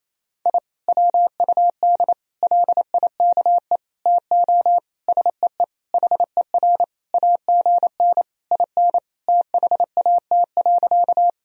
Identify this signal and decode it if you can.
Morse code, heard as a keyed tone, about 28 wpm, 700 Hz. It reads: I WUD LIKE TO SEE HER AGN IN THAT.
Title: Sentences from Top 200 Words and Abbreviations 28wpm